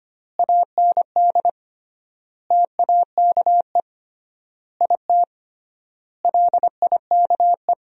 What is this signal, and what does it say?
Morse code, heard as a keyed tone, about 25 wpm, 700 Hz. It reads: AND TAKE IT LIKE
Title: 3x Word-Spacing - Sentences from Top 300 Words - 25wpm